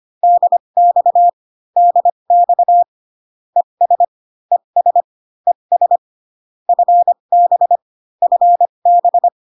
Morse code, heard as a keyed tone, about 25 wpm, spaced as for 22 wpm, 700 Hz. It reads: DX DX ES ES ES FB FB
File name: CW common abbreviations alphabetical repeated 3 times 25 18wpm